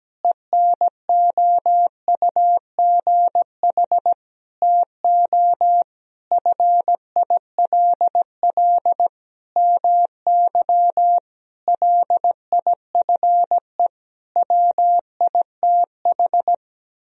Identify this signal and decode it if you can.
Morse code, heard as a keyed tone, about 17 wpm, 700 Hz. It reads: ENOUGH TO FILL MY LIFE WITH
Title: Sentences from Top 1000 Words - Review Part 3 - 17wpm